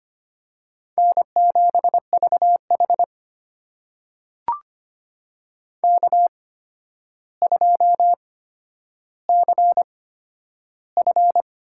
Morse code, heard as a keyed tone, about 25 wpm, spaced as for 11 wpm, 700 Hz. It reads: N7VH K2CF
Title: USA Call Signs - 4 Characters - 8x Character Spacing - 25wpm